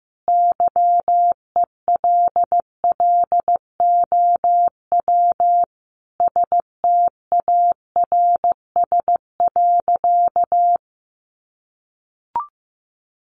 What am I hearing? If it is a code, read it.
Morse code, heard as a keyed tone, about 15 wpm, 700 Hz. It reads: YELLOW STARS.
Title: Additional Easy Phrases 15wpm